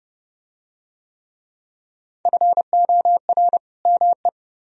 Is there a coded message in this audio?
Yes: Morse code from 2.3 to 4.3 s, heard as a keyed tone, about 28 wpm, 700 Hz. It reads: FOR ME